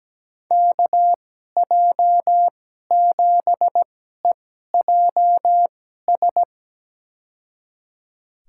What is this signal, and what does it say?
Morse code, heard as a keyed tone, about 17 wpm, spaced as for 12 wpm, 700 Hz. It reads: KJ7EJS
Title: USA Call Signs - 6 Characters - 2x Character Spacing - 17wpm